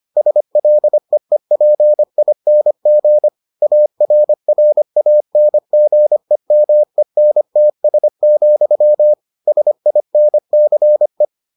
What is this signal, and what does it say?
Morse code, heard as a keyed tone, about 25 wpm, 600 Hz. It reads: SLEEPING ARRANGEMENTS, SINCE